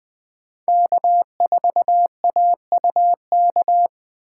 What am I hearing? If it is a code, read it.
Morse code, heard as a keyed tone, about 24 wpm, 700 Hz. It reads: K4AUK
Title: Speed Racing - USA 1x3 Call Signs 20wpm